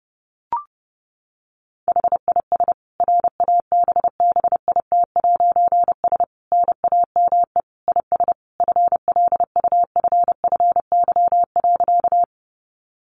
Morse code, heard as a keyed tone, about 30 wpm, 700 Hz. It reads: HIS RABBIT'S NAME IS FLUFFY.
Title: Easy Phrases 30wpm